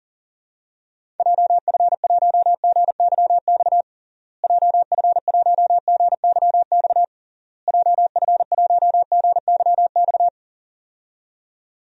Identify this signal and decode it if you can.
Morse code, heard as a keyed tone, about 40 wpm, 700 Hz. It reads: JF1GYX JF1GYX JF1GYX
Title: DXCC-100 Call Signs - 6 Characters - Repeated 3x - 40wpm